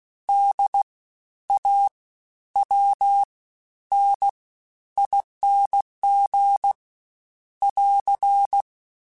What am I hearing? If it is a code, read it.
Morse code, heard as a keyed tone, about 16 wpm, spaced as for 11 wpm, 800 Hz. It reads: DAWNING <AR>
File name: Special INGs-mp3s / Special INGs-mp3s / Fast ing @15